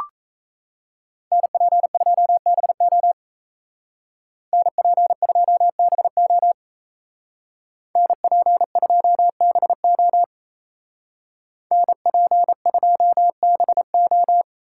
Morse code, heard as a keyed tone, about 35 wpm, 700 Hz. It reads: NP2BO NP2BO NP2BO NP2BO